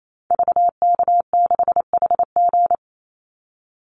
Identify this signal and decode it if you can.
Morse code, heard as a keyed tone, about 28 wpm, 700 Hz. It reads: VK6HG
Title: Call Signs Repeated 2x - 28wpm